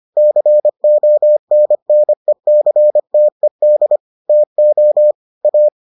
Morse code, heard as a keyed tone, about 25 wpm, 600 Hz. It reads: CONNECTED TO A